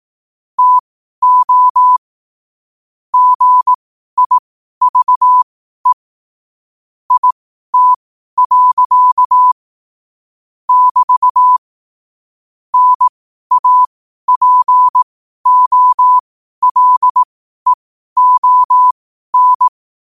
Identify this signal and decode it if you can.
Morse code, heard as a keyed tone, about 18 wpm, spaced as for 12 wpm, 1000 Hz. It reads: TO GIVE IT. <BT> NAPOLEON
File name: qotdfiles / 12wpm